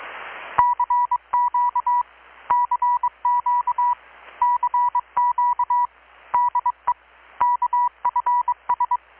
Morse code, heard as a keyed tone, about 22 wpm, 980 Hz, with radio noise. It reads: CQ CQ CQ DE KFS
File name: blog / swl / 6-abril-2019-12695-USB-KPS